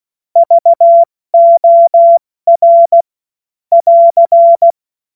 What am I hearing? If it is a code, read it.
Morse code, heard as a keyed tone, about 16 wpm, 680 Hz. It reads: VOR <AR>
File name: Morsekurs / Witze / Witz 10.Oktober 2023 70 F80